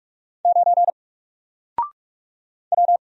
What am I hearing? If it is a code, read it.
Morse code, heard as a keyed tone, about 44 wpm, 700 Hz. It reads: 9 W
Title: Single Letter-Number 45wpm